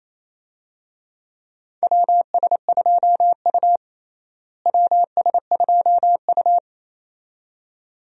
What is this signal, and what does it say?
Morse code, heard as a keyed tone, about 28 wpm, 700 Hz. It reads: WS2U WS2U
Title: USA Call Signs - 4 Characters - Repeated 2x - 28wpm